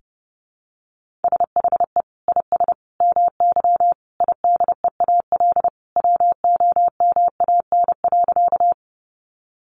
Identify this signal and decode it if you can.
Morse code, heard as a keyed tone, about 28 wpm, 700 Hz. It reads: SHE IS MY IDEAL WOMAN.
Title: Sentences from Top 1400 Words 30wpm